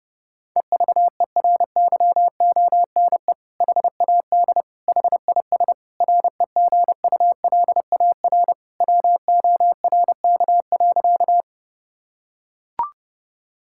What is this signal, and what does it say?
Morse code, heard as a keyed tone, about 32 wpm, 700 Hz. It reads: EVERYONE HAD HIS REGULAR WORK.
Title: Sentences from Top 1000 Words - Review Part 6 - 30wpm